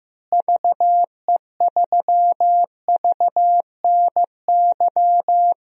Morse code, heard as a keyed tone, about 15 wpm, 700 Hz. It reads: VE3VNY
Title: DXCC-100 Call Signs - 6 Characters - Repeated 3x - 15wpm